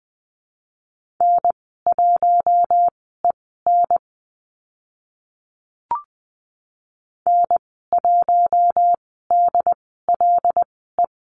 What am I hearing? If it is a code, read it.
Morse code, heard as a keyed tone, about 20 wpm, spaced as for 14 wpm, 700 Hz. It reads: N1EN N1DLE
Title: 082 Lesson 14g - Number 2 - Call Signs 20wpm